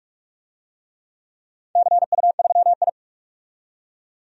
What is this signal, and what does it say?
Morse code, heard as a keyed tone, about 44 wpm, 700 Hz. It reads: CU3I
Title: DXCC-100 Call Signs - 4 Characters - 45wpm